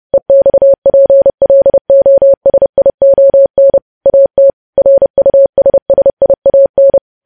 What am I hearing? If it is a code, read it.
Morse code, heard as a keyed tone, about 30 wpm, 550 Hz. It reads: EXPLOSION AT RUSSIAN
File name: podcast / News headlines-1x-30x30